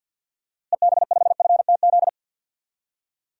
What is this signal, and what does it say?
Morse code, heard as a keyed tone, about 50 wpm, 700 Hz. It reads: EB5FTZ
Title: DXCC-100 Call Signs - 6 Characters - Repeated 2x - 50wpm